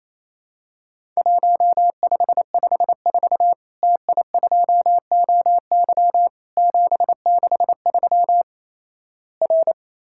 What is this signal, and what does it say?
Morse code, heard as a keyed tone, about 28 wpm, 700 Hz. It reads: 1554 TI2OY 763 R
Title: Worked All Europe DX Contest - QTC Exchange - 28wpm